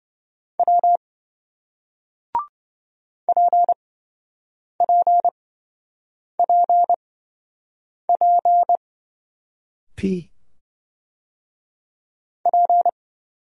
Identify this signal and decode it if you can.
Morse code, heard as a keyed tone, about 27 wpm, spaced as for 7 wpm, 700 Hz. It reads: W PPPP P